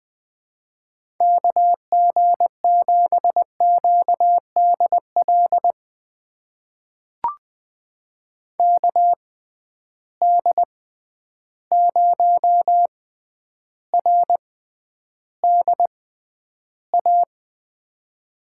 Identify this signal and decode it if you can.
Morse code, heard as a keyed tone, about 20 wpm, spaced as for 10 wpm, 700 Hz. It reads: KG7QDL KD0RDA
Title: USA Call Signs - 6 Characters - 6x Character Spacing - 20wpm